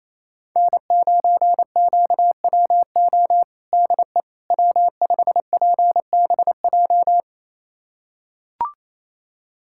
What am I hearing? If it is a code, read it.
Morse code, heard as a keyed tone, about 27 wpm, 700 Hz. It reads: N9QWO DE W5PBJ
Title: USA Callsign DE Callsign - 5 Characters - 28wpm